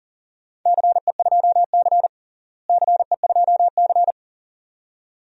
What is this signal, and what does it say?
Morse code, heard as a keyed tone, about 40 wpm, 700 Hz. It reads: CE2C CE2C
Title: DXCC-100 Call Signs - 4 Characters - Repeated 2x - 40wpm